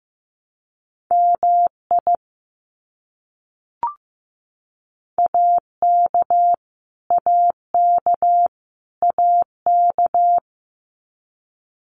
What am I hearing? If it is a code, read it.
Morse code, heard as a keyed tone, about 15 wpm, 700 Hz. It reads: MI AK AK AK
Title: US States Repeated 3x 15wpm